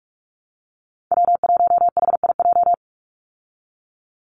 Morse code, heard as a keyed tone, about 44 wpm, 700 Hz. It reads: W1HIJ
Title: Call Signs Repeated 3x - 45wpm